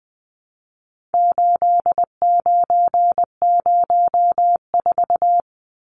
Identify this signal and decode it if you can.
Morse code, heard as a keyed tone, about 20 wpm, 700 Hz. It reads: 8904
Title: Numbers - 4-Digit 20wpm